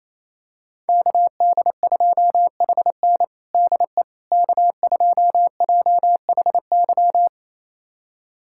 Morse code, heard as a keyed tone, about 28 wpm, 700 Hz. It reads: KD2HN DE K2JHY
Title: USA Callsign DE Callsign - 5 Characters - 28wpm